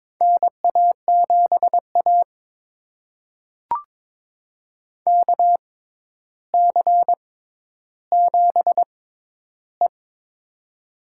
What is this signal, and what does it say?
Morse code, heard as a keyed tone, about 23 wpm, spaced as for 11 wpm, 700 Hz. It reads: NA7A KC7E
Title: USA Call Signs - 4 Characters - 6x Character Spacing - 22wpm